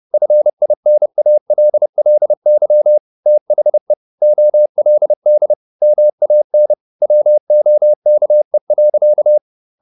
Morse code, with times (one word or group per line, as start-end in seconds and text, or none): 0.1-3.0 FINALLY
3.3-3.9 THE
4.2-5.5 OLD
5.8-6.7 MAN
7.0-9.4 WOKE.